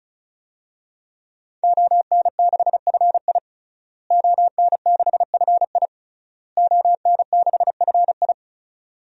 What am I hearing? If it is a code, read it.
Morse code, heard as a keyed tone, about 36 wpm, 700 Hz. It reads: ON6FI ON6FI ON6FI
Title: DXCC-100 Call Signs - 5 Characters - Repeated 3x - 35wpm